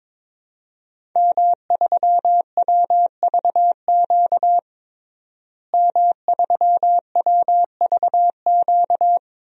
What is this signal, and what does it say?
Morse code, heard as a keyed tone, about 22 wpm, 700 Hz. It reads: M3WVQ M3WVQ
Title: DXCC-100 Call Signs - 5 Characters - Repeated 2x - 22wpm